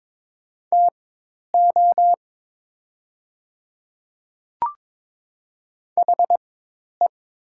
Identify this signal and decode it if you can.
Morse code, heard as a keyed tone, about 22 wpm, spaced as for 10 wpm, 700 Hz. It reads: TO HE
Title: 4x Character Spacing - Top 200 Words - 22wpm